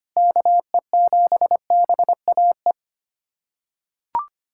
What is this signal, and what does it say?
Morse code, heard as a keyed tone, about 25 wpm, 700 Hz. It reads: KE7BAE
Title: USA Callsign DE Callsign 25wpm